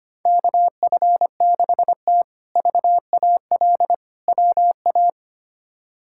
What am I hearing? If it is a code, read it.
Morse code, heard as a keyed tone, about 25 wpm, 700 Hz. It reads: KF6T VAL WA
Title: K1USN SST QSO Practice - 25wpm